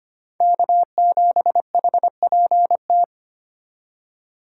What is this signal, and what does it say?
Morse code, heard as a keyed tone, about 25 wpm, 700 Hz. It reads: K7HPT
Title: K1USN SST QSO Practice - 25wpm